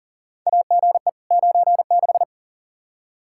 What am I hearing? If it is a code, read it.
Morse code, heard as a keyed tone, about 40 wpm, 700 Hz. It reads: AGE 96
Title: Practice QSO - Exchange 3 - More Info - 40wpm